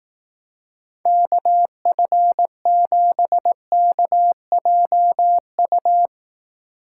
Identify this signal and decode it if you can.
Morse code, heard as a keyed tone, about 18 wpm, 700 Hz. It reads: KF7KJU
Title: Speed Racing - USA Call Signs 15wpm